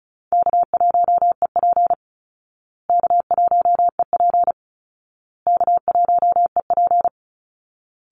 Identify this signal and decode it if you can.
Morse code, heard as a keyed tone, about 34 wpm, 700 Hz. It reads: K1EP K1EP K1EP
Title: Easy Call Signs Repeated 3x - 35wpm